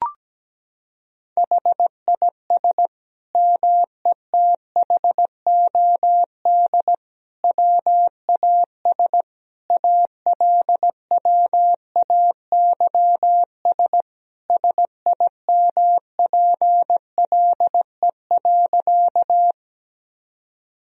Morse code, heard as a keyed tone, about 17 wpm, 700 Hz. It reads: HIS METHOD WAS ALWAYS SIMPLE.
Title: Sentences from Top 1000 Words - Review Part 3 - 17wpm